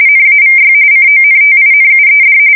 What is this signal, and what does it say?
RTTY, about 45.45 baud. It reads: NAVY CONTINUED
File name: fsk170 / usnavy1